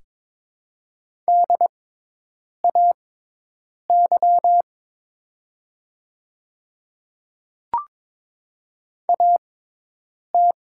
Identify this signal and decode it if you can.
Morse code, heard as a keyed tone, about 22 wpm, spaced as for 8 wpm, 700 Hz. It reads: DAY AT
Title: 6x Character Spacing - Top 100 Words - 22wpm